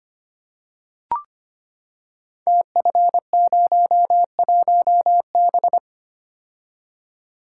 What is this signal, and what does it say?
Morse code, heard as a keyed tone, about 25 wpm, 700 Hz. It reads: TF01B
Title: Code Groups - 5 Letter:Numbers - 25wpm